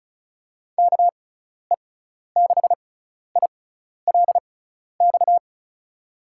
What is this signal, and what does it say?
Morse code, heard as a keyed tone, about 36 wpm, spaced as for 12 wpm, 700 Hz. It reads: KE6ILX
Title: USA Call Signs - 6 Characters - 6x Character Spacing - 35wpm